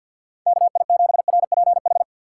Morse code, heard as a keyed tone, about 52 wpm, 700 Hz. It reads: KI7RPH
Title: USA Call Signs - 3x Character Spacing - 50wpm